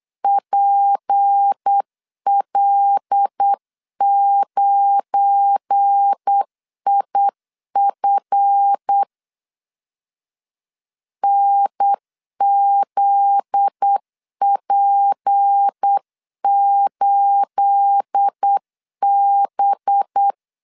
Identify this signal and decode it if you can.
Morse code, heard as a keyed tone, about 8 wpm, 800 Hz. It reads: PL9IF NZP8B
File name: CW / DET / 1 9 1500C S A 8wpm 5637